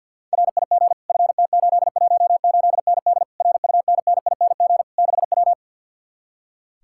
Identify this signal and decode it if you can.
Morse code, heard as a keyed tone, about 63 wpm, 700 Hz. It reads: RIG FT818ND RUNNING 6W